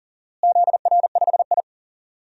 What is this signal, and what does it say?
Morse code, heard as a keyed tone, about 40 wpm, 700 Hz. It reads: ZR5I
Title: DXCC-100 Call Signs - 4 Characters - Repeated 2x - 40wpm